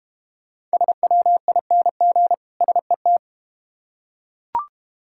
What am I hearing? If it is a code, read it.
Morse code, heard as a keyed tone, about 32 wpm, 700 Hz. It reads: SWING SET